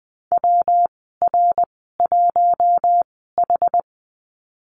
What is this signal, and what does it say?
Morse code, heard as a keyed tone, about 19 wpm, spaced as for 14 wpm, 700 Hz. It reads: WR1H
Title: 193 Lesson 31g - Letter v - Call Signs 20wpm